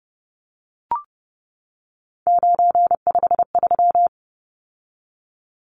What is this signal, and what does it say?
Morse code, heard as a keyed tone, about 30 wpm, 700 Hz. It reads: 953
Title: Numbers - 3-Digit 30wpm